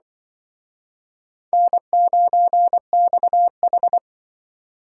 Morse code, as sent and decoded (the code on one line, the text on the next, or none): -. ----. -..- ....
N9XH